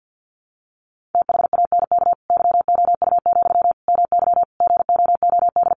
Morse code, heard as a keyed tone, about 49 wpm, 700 Hz. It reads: THANK YOU, MY GOOD
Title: Sentences from Top 600 Words 50wpm